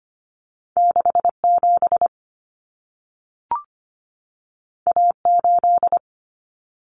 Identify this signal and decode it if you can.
Morse code, heard as a keyed tone, about 25 wpm, 700 Hz. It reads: 67 A8
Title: Common Cut Numbers - 2-Digit 25wpm